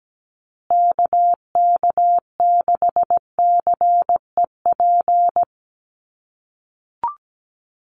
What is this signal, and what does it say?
Morse code, heard as a keyed tone, about 17 wpm, 700 Hz. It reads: KK6CEP
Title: USA Call Signs Repeated 3x 17wpm